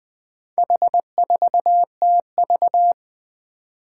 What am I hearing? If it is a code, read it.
Morse code, heard as a keyed tone, about 20 wpm, 700 Hz. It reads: H4TV
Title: Four Challenging Letter-Number Combinations 20wpm